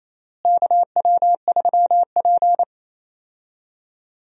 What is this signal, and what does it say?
Morse code, heard as a keyed tone, about 27 wpm, 700 Hz. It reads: KW3P
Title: USA Call Signs - 4 Characters - Repeated 3x - 28wpm